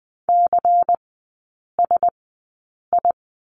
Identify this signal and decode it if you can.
Morse code, heard as a keyed tone, about 20 wpm, spaced as for 8 wpm, 690 Hz. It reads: CSI
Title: 145 Lesson 24e - Letter y - Three Characters 20wpm